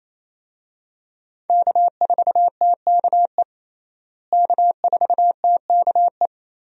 Morse code, heard as a keyed tone, about 28 wpm, 700 Hz. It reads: K4TKE K4TKE
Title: USA Call Signs - 5 Characters - Repeated 3x - 28wpm